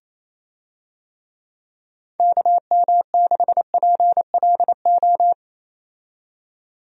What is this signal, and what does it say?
Morse code, heard as a keyed tone, about 28 wpm, 700 Hz. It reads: KM6PLO
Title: USA Call Signs - 6 Characters - 28wpm